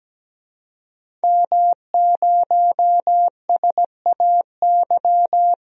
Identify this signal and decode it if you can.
Morse code, heard as a keyed tone, about 17 wpm, 700 Hz. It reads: M0SAY
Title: DXCC-100 Call Signs - 5 Characters - 17wpm